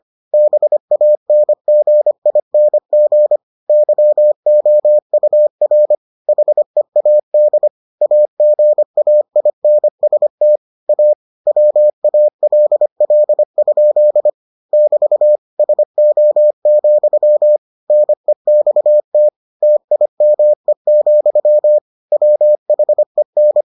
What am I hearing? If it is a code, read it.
Morse code, heard as a keyed tone, about 25 wpm, 600 Hz. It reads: BANGING YOUR HEAD AGAINST A WALL? <BT> SO, NEXT TIME, WHEN